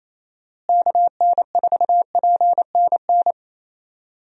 Morse code, heard as a keyed tone, about 28 wpm, 700 Hz. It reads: KN4PNN
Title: USA Call Signs - 6 Characters - Repeated 3x - 28wpm